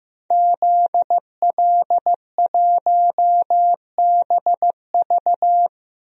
Morse code, heard as a keyed tone, about 15 wpm, 700 Hz. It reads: ZL1BV
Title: DXCC-100 Call Signs - 5 Characters - 15wpm